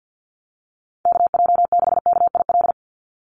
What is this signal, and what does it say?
Morse code, heard as a keyed tone, about 50 wpm, 700 Hz. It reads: KJ6KIL
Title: USA Call Signs 50wpm